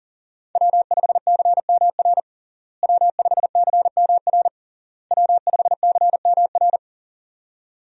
Morse code, heard as a keyed tone, about 40 wpm, 700 Hz. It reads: W5CMR W5CMR W5CMR
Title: USA Call Signs - 5 Characters - Repeated 3x - 40wpm